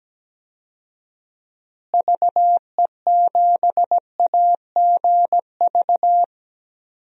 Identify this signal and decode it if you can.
Morse code, heard as a keyed tone, about 17 wpm, 700 Hz. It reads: VE7AGV
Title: DXCC-100 Call Signs - 6 Characters - Repeated 3x - 17wpm